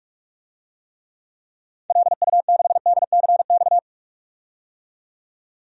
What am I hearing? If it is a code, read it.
Morse code, heard as a keyed tone, about 41 wpm, 700 Hz. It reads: LU6DCX
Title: DXCC-100 Call Signs - 6 Characters - 45wpm